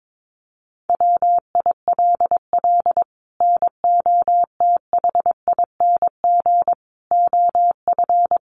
Morse code, heard as a keyed tone, about 22 wpm, 700 Hz. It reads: WILL NOTHING OF
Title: Sentences from Top 1200 Words 22wpm